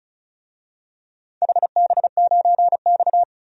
Morse code, heard as a keyed tone, about 36 wpm, 700 Hz. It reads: HB9X
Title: DXCC-100 Call Signs - 4 Characters - 35wpm